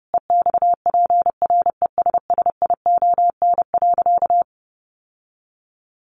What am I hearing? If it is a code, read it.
Morse code, heard as a keyed tone, about 30 wpm, 700 Hz. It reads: EXPRESSION.
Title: Sentences from Top 1200 Words 30wpm